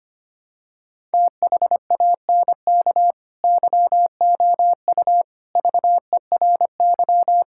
Morse code, heard as a keyed tone, about 25 wpm, 700 Hz. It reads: THANK YOU VERY
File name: practice / 25wpm / Sentences with 3-Word Phrases - 25wpm